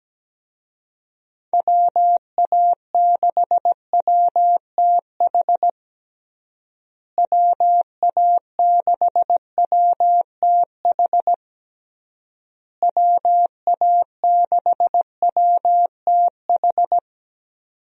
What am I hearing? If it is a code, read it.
Morse code, heard as a keyed tone, about 17 wpm, 700 Hz. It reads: WA6WTH WA6WTH WA6WTH
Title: USA Call Signs - 6 Characters - Repeated 3x - 17wpm